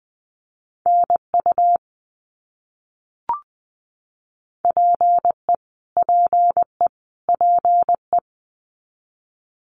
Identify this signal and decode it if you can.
Morse code, heard as a keyed tone, about 20 wpm, 700 Hz. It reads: NU PE PE PE
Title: Canadian Provinces and Territories - Repeated 3x - 20wpm